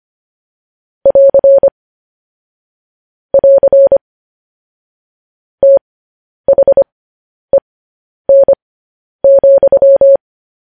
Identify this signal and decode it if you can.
Morse code, heard as a keyed tone, about 25 wpm, spaced as for 10 wpm, 550 Hz. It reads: <AR> <AR> THEN,
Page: news-headlines-in-morse-code-at-10-wpm